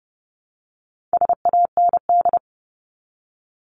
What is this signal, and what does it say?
Morse code, heard as a keyed tone, about 30 wpm, 700 Hz. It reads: SAND
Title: Top 1501-1750 Words 30wpm